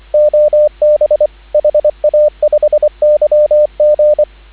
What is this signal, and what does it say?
Morse code, heard as a keyed tone, about 25 wpm, 590 Hz, with radio noise. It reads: OB HA5YG
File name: CW / 2017 HA-OB HA5YG